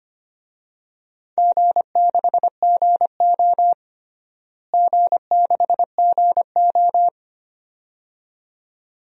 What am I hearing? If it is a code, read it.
Morse code, heard as a keyed tone, about 25 wpm, 700 Hz. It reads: G6GO G6GO
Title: DXCC-100 Call Signs - 4 Characters - Repeated 2x - 25wpm